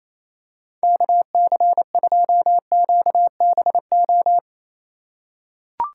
Morse code, heard as a keyed tone, about 28 wpm, 700 Hz. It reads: KC2QBO
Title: USA Call Signs - 6 Characters - 3x Character Spacing - 28wpm